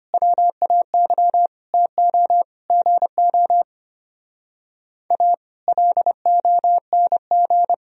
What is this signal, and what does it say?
Morse code, heard as a keyed tone, about 29 wpm, 700 Hz. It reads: WAY TO GO A LONG
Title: Speed-Racing - Sets of 5 Words 25wpm